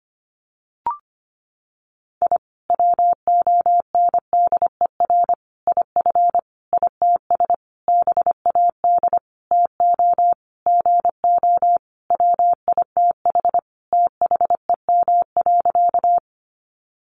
Morse code, heard as a keyed tone, about 25 wpm, 700 Hz. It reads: I WONDER IF ITS BAD TO GO WITH THEM.
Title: Sentences from Top 700 Words 25wpm